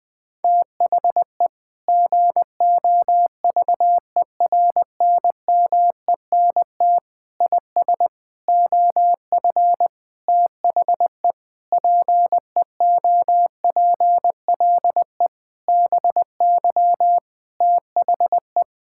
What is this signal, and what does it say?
Morse code, heard as a keyed tone, about 21 wpm, 700 Hz. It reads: THE GOVERNMENT IS OF THE PEOPLE BY THE
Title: Sentences with 9-Word Phrases - 20wpm